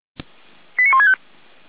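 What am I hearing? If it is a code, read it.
Five-tone selective call (ZVEI), heard as quick runs of five tones, tones of about 70 ms. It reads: 89165
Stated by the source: ZVEI five-tone selective call received on an SDR.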